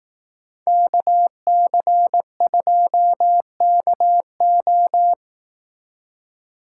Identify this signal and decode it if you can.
Morse code, heard as a keyed tone, about 18 wpm, 700 Hz. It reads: KC2KO
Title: Speed Racing - USA 2x2 Call Signs 15wpm